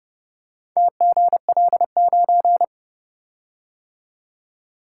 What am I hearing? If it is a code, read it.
Morse code, heard as a keyed tone, about 29 wpm, 700 Hz. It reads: TGL9